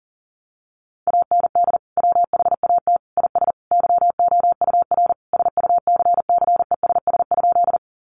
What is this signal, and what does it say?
Morse code, heard as a keyed tone, about 40 wpm, 700 Hz. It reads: AND WHAT IS YOUR SUCCESS?
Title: Sentences from Top 900 Words 40wpm